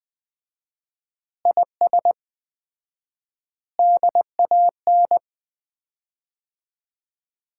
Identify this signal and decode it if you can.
Morse code, heard as a keyed tone, about 20 wpm, 700 Hz. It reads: IS DAN
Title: Common QSO Phrases - 4x Word-Spacing - 20wpm